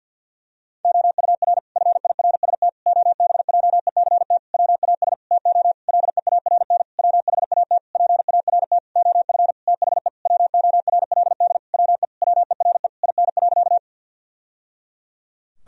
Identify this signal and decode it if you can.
Morse code, heard as a keyed tone, about 50 wpm, 700 Hz. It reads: OUR FIRST OBJECT WAS TO LEARN WHAT PART OF THE WORLD WE WERE IN.
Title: Sentences from Top 1000 Words - Review Part 3 - 50wpm